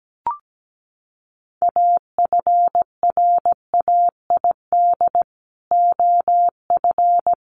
Morse code, heard as a keyed tone, about 17 wpm, 700 Hz. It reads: AFRAID OF
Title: Sentences from Top 1500 Words 17wpm